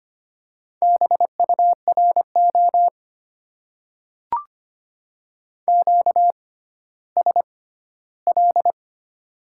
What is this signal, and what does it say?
Morse code, heard as a keyed tone, about 25 wpm, spaced as for 14 wpm, 700 Hz. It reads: BURO QSL
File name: Top 90 QSO Elements - 6x Character Spacing - 25wpm